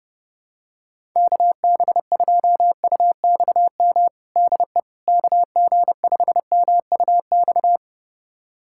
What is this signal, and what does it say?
Morse code, heard as a keyed tone, about 29 wpm, 700 Hz. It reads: KB2UXM DE KG5MUX
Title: USA Callsign DE Callsign - 6 Characters - 30wpm